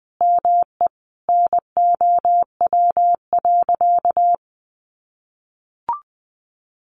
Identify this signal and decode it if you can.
Morse code, heard as a keyed tone, about 20 wpm, 700 Hz. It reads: ME NOW.
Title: Sentences from Top 1100 Words 20wpm